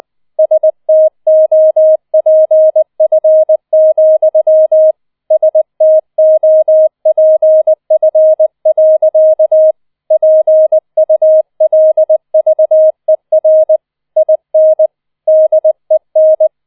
Morse code, heard as a keyed tone, about 20 wpm, 630 Hz. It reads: STOPF, STOPF. PULVER IN DEN